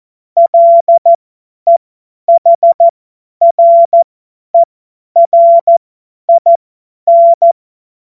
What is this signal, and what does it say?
Morse code, heard as a keyed tone, about 14 wpm, spaced as for 10 wpm, 680 Hz. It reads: LEHRERIN
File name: Morsekurs / Witze / Witz 28. September 2021, 50 F70